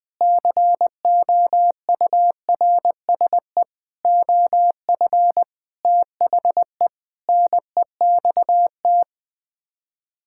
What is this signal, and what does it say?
Morse code, heard as a keyed tone, about 20 wpm, 700 Hz. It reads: COURSE OF THE NEXT
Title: Sets of 6 Words - 20wpm